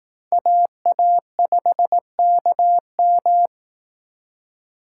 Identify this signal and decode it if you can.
Morse code, heard as a keyed tone, about 18 wpm, 700 Hz. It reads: AA5KM
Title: Speed Racing - USA 2x2 Call Signs 15wpm